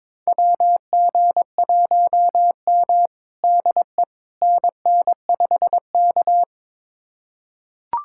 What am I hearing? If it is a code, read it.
Morse code, heard as a keyed tone, about 22 wpm, 700 Hz. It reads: WG1M DE NN5K
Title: USA Callsign DE Callsign - 4 Characters - 22wpm